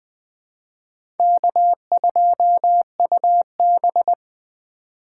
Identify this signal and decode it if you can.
Morse code, heard as a keyed tone, about 20 wpm, 700 Hz. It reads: K2UB